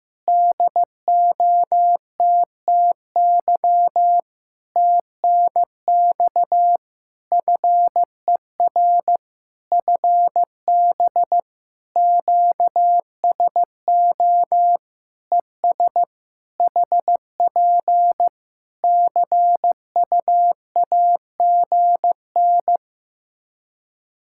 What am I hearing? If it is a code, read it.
Morse code, heard as a keyed tone, about 15 wpm, 700 Hz. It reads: DOTTY TNX FER FB QSO ES HP CUAGN
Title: K1USN SST QSO Practice 15wpm